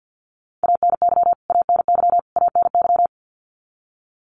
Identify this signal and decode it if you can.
Morse code, heard as a keyed tone, about 50 wpm, 700 Hz. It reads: ANY ANY ANY
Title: Top 200 Words Repeated 50wpm